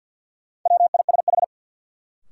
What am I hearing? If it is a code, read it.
Morse code, heard as a keyed tone, about 52 wpm, 700 Hz. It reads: WISH
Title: Sentences from Top 1600 Words - New Word Drill - 50wpm